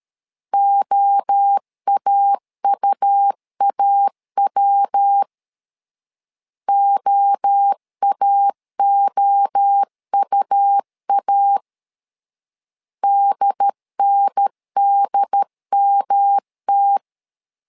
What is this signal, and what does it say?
Morse code, heard as a keyed tone, about 13 wpm, 800 Hz. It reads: OAUAW OAOUA DNDMT
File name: LEZIONI / 3 MWRO / 3 MWRO 12wpm 1730